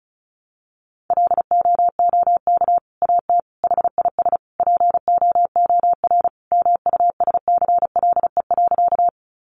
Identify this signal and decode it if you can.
Morse code, heard as a keyed tone, about 34 wpm, 700 Hz. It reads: LOOK AT HIS POOR MUSCLE.
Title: Sentences from Top 1300 Words 35wpm